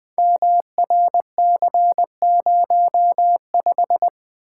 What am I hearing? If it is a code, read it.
Morse code, heard as a keyed tone, about 20 wpm, 700 Hz. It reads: MRC05
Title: Code Groups - 5 Letter:Numbers - 20wpm